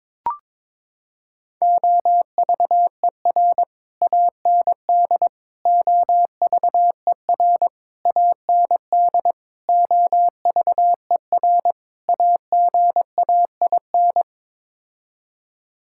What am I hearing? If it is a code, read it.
Morse code, heard as a keyed tone, about 22 wpm, 700 Hz. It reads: OVER AND OVER AND OVER AGAIN
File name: practice / 22wpm / Sets of 6 Words - 22wpm